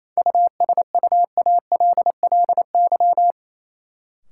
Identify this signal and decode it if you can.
Morse code, heard as a keyed tone, about 28 wpm, 700 Hz. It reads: USUALLY
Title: Sentences from Top 1000 Words - Review Part 3 - New Word Drill - 28wpm